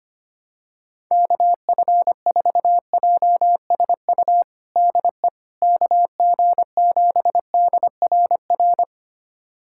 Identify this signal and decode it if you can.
Morse code, heard as a keyed tone, about 25 wpm, 700 Hz. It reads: KF4JSU DE KG7DRR
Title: USA Callsign DE Callsign 25wpm